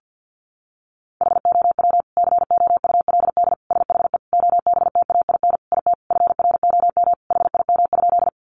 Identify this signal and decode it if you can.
Morse code, heard as a keyed tone, about 50 wpm, 700 Hz. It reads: HOW COULD SHE OBTAIN IT FROM HIM?
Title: Sentences from Top 1300 Words 50wpm